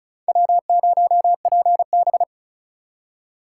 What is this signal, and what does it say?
Morse code, heard as a keyed tone, about 36 wpm, 700 Hz. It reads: W0PB
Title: USA Call Signs - 2x Character Spacing - 35wpm